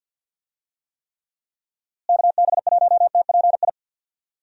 Morse code, heard as a keyed tone, about 52 wpm, 700 Hz. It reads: KB1TPI
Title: USA Call Signs - 6 Characters - Repeated 3x - 50wpm